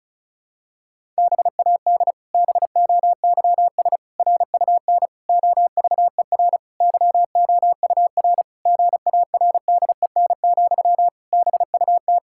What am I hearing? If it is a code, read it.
Morse code, heard as a keyed tone, about 36 wpm, 700 Hz. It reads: BAD BOYS RUN OVER YOUR GARDEN, BUT
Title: Pangram Challenge v2 - 35wpm